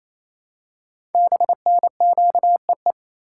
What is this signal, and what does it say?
Morse code, heard as a keyed tone, about 27 wpm, 700 Hz. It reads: BNQEE
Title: Code Groups - 5 Letters - 28wpm